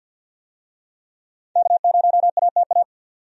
Morse code, heard as a keyed tone, about 50 wpm, 700 Hz. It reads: K0ATA
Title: USA Call Signs - 5 Characters - 50wpm